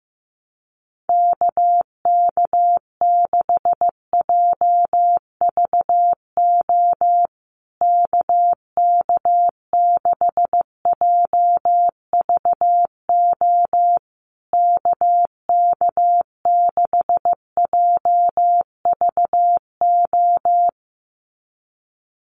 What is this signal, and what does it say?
Morse code, heard as a keyed tone, about 15 wpm, 700 Hz. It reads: KK6JVO KK6JVO KK6JVO
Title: USA Call Signs Repeated 3x 15wpm